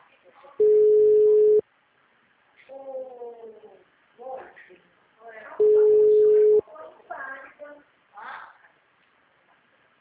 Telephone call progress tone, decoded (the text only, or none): ringback tone